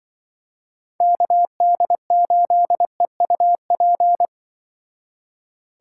Morse code, heard as a keyed tone, about 23 wpm, 700 Hz. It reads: KD8EUP
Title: Speed Racing - USA 2x3 Call Signs 20wpm